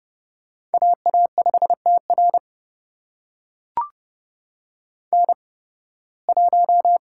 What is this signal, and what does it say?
Morse code, heard as a keyed tone, about 32 wpm, spaced as for 20 wpm, 700 Hz. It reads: AA5TR N1
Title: USA Call Signs - 8x Character Spacing - 30wpm